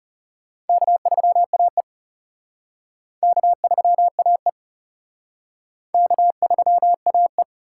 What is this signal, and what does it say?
Morse code, heard as a keyed tone, about 37 wpm, 700 Hz. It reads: K3AE K3AE K3AE
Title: Speed Racing - USA 1x2 Call Signs 30wpm